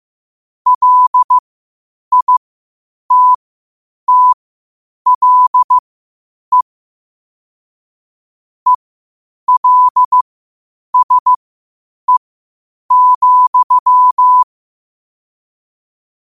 Morse code, heard as a keyed tone, about 15 wpm, spaced as for 8 wpm, 1000 Hz. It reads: LITTLE ELSE,
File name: 8wpm